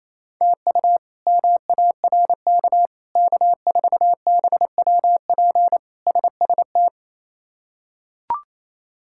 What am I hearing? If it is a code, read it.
Morse code, heard as a keyed tone, about 27 wpm, 700 Hz. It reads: TU MARK K4BWP SST
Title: K1USN SST QSO Practice - 28wpm